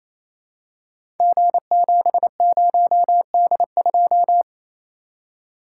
Morse code, heard as a keyed tone, about 28 wpm, 700 Hz. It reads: G70D2
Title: Code Groups - 5 Letter:Numbers - 28wpm